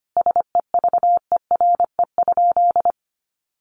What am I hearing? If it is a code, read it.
Morse code, heard as a keyed tone, about 25 wpm, 700 Hz. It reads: SEVERE?
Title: Sentences from Top 1400 Words 25wpm